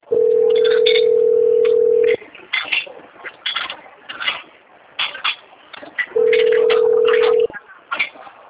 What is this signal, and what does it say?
Ringback tone (North America), 440 + 480 Hz.
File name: out- - - - - .10621